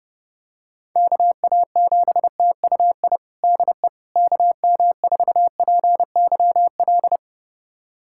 Morse code, heard as a keyed tone, about 28 wpm, 700 Hz. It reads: KA7TUI DE KM4PYL
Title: USA Callsign DE Callsign 30wpm